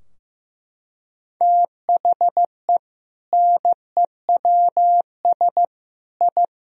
Morse code, heard as a keyed tone, about 15 wpm, 700 Hz. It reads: THE NEWS I
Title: Sentences from Top 1000 Words - Review Part 3 - 15wpm